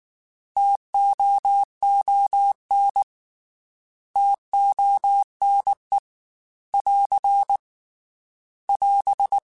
Morse code, heard as a keyed tone, about 19 wpm, 800 Hz. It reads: TOON TONE <AR> <AS>
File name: T Prefixes / T mp3s